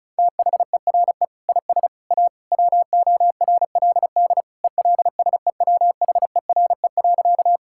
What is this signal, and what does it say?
Morse code, heard as a keyed tone, about 34 wpm, 700 Hz. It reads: THERE IS A WORLD ELSEWHERE.
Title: Sentences from Top 1600 Words 35wpm